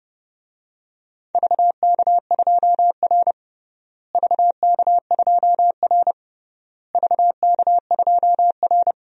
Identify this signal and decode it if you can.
Morse code, heard as a keyed tone, about 30 wpm, 700 Hz. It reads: VK2R VK2R VK2R
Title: DXCC-100 Call Signs - 4 Characters - Repeated 3x - 30wpm